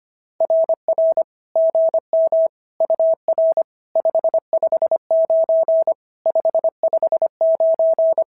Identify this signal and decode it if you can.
Morse code, heard as a keyed tone, about 25 wpm, 650 Hz. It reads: RR GM UR 559 559
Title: SOTA QSO Practice 25wpm